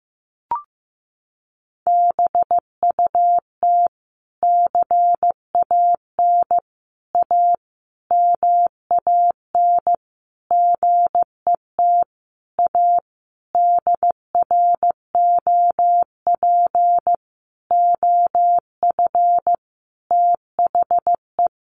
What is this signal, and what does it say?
Morse code, heard as a keyed tone, about 15 wpm, 700 Hz. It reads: BUT CAN A MAN GET A DROP OF THE
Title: Sentences from Top 1400 Words 15wpm